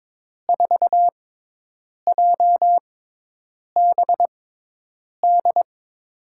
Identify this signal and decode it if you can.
Morse code, heard as a keyed tone, about 22 wpm, spaced as for 8 wpm, 700 Hz. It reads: 4JBD
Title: USA Call Signs - 5 Characters - 6x Character Spacing - 22wpm